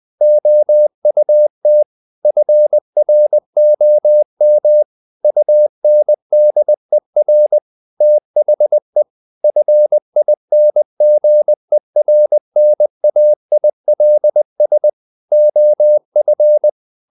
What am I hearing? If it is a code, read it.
Morse code, heard as a keyed tone, about 20 wpm, 600 Hz. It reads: OUT FROM UNDER THE FINGERNAILS OF
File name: the old man and the sea part 2 20wpm